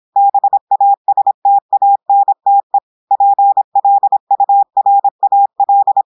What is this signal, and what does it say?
Morse code, heard as a keyed tone, about 26 wpm, 800 Hz. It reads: BASTANTE PLURAL
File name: audios / portugais / portugais texte en clair 1 vitesse 26 mots minutes